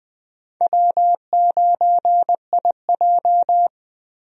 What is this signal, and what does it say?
Morse code, heard as a keyed tone, about 21 wpm, 700 Hz. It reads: W9IJ
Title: USA Call Signs - 4 Characters - 20wpm